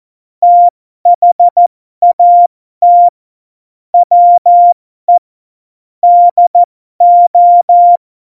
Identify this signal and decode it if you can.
Morse code, heard as a keyed tone, about 14 wpm, spaced as for 12 wpm, 700 Hz. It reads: THAT WE DO
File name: Morsekurs / Witze